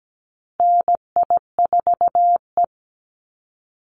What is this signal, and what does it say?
Morse code, heard as a keyed tone, about 17 wpm, 700 Hz. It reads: NI4E
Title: Call Signs Repeated 2x - 17wpm